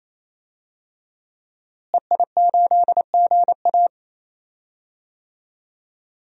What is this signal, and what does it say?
Morse code, heard as a keyed tone, about 28 wpm, 700 Hz. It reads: EI8GA
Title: DXCC-100 Call Signs - 5 Characters - Repeated 3x - 28wpm